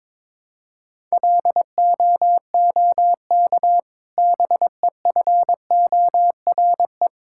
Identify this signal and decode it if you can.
Morse code, heard as a keyed tone, about 22 wpm, 700 Hz. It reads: LOOK BEFORE
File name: Difficult Sentences with Hints - 22wpm